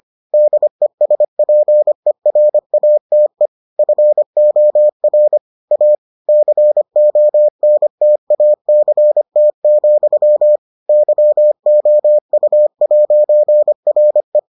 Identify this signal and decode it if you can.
Morse code, heard as a keyed tone, about 25 wpm, 600 Hz. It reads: DESPERATE FOR A CONTACT, YOU'RE